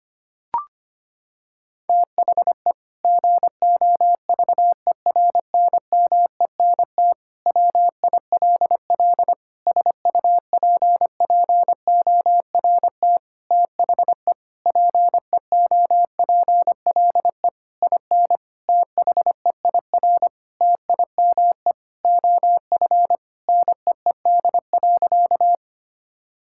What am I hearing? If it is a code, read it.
Morse code, heard as a keyed tone, about 25 wpm, 700 Hz. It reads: THE GOVERNMENT WILL SUPPORT THE PEOPLE IN THEIR TIME OF NEED.
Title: Sentences from Top 1000 Words - Review Part 5 - 25wpm